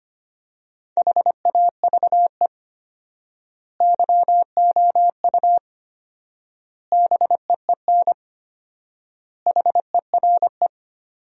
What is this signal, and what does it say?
Morse code, heard as a keyed tone, about 25 wpm, 700 Hz. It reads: HAVE YOU BEEN HERE
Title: 4x Word-Spacing - Sentences from Top 200 Words - 25wpm